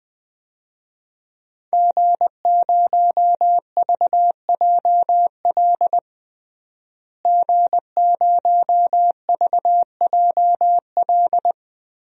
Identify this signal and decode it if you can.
Morse code, heard as a keyed tone, about 20 wpm, 700 Hz. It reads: G0VJL G0VJL
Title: DXCC-100 Call Signs - 5 Characters - Repeated 3x - 20wpm